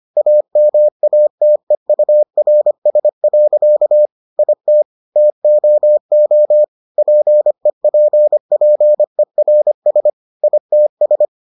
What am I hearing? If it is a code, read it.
Morse code, heard as a keyed tone, about 25 wpm, 600 Hz. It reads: AMATEURS. IT TOO PEPPERS ITS